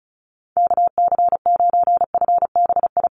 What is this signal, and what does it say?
Morse code, heard as a keyed tone, about 34 wpm, 700 Hz. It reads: KC9FBI
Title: USA Call Signs Repeated 2x 35wpm